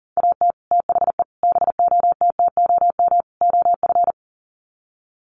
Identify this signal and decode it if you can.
Morse code, heard as a keyed tone, about 40 wpm, 700 Hz. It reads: AT THE BOTTOM OF
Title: Sets of 4 Words 40wpm